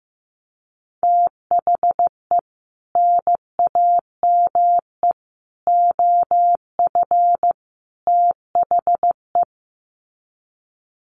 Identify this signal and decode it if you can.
Morse code, heard as a keyed tone, about 15 wpm, 700 Hz. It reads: THE NAME OF THE
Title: Sets of 4 Words 15wpm